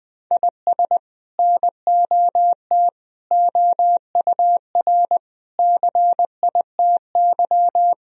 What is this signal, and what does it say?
Morse code, heard as a keyed tone, about 20 wpm, 700 Hz. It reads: IS NOT OUR CITY
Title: Sentences from Top 1000 Words - Review Part 3 - 20wpm